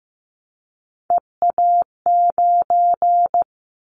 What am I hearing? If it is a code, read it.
Morse code, heard as a keyed tone, about 15 wpm, 700 Hz. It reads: EA9
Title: Callsign Prefixes for Top 100 DXCC - 15wpm